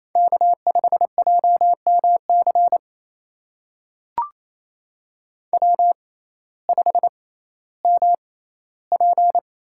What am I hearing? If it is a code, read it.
Morse code, heard as a keyed tone, about 27 wpm, spaced as for 15 wpm, 700 Hz. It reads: K5JMC W5MP
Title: USA Call Signs - 6x Character Spacing - 28wpm